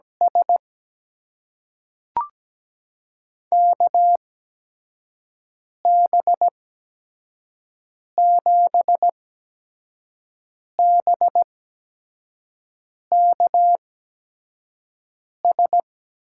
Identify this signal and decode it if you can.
Morse code, heard as a keyed tone, about 17 wpm, spaced as for 5 wpm, 700 Hz. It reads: S KB7BKS
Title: USA Call Signs - 6 Characters - 8x Character Spacing - 17wpm